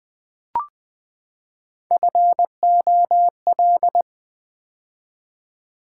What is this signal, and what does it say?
Morse code, heard as a keyed tone, about 20 wpm, 700 Hz. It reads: FOL
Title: Build A Word - 3 Letters At A Time - 6-Letter Words - 20wpm